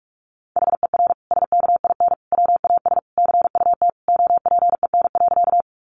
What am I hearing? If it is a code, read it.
Morse code, heard as a keyed tone, about 46 wpm, 700 Hz. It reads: HER SKIN WAS CUT OPEN.
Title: Sentences from Top 1300 Words 45wpm